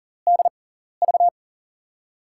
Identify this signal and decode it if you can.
Morse code, heard as a keyed tone, about 40 wpm, spaced as for 14 wpm, 700 Hz. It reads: DV
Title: USA Call Signs - 6 Characters - 6x Character Spacing - 40wpm